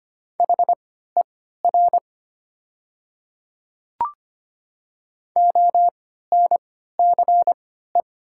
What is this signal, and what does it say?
Morse code, heard as a keyed tone, about 25 wpm, spaced as for 14 wpm, 700 Hz. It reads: HER ONCE
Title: 3x Character Spacing - Top 300 Words - 25wpm